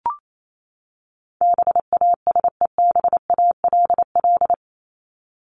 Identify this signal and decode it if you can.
Morse code, heard as a keyed tone, about 28 wpm, 700 Hz. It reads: BASEBALL
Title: Top 1751-2000 Words 28wpm